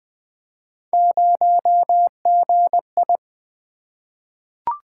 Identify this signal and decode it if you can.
Morse code, heard as a keyed tone, about 20 wpm, 700 Hz. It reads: 0GI
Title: Three Letter-Number Combinations - 20wpm